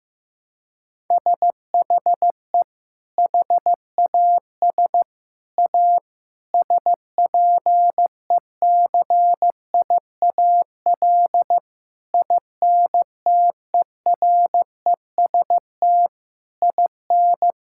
Morse code, heard as a keyed tone, about 15 wpm, 700 Hz. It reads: SHE HAS A SPECIAL INTEREST IN
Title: Sentences from Top 1000 Words - Review Part 5 - 15wpm